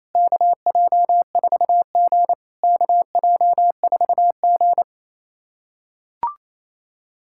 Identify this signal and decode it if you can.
Morse code, heard as a keyed tone, about 28 wpm, 690 Hz. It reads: KJ4G KJ4G
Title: SOTA QSO Practice 28wpm